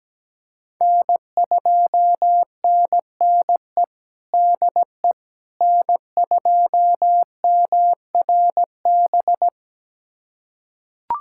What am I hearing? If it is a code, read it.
Morse code, heard as a keyed tone, about 17 wpm, 700 Hz. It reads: N2NNE DE N2MRB
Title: USA Callsign DE Callsign - 5 Characters - 17wpm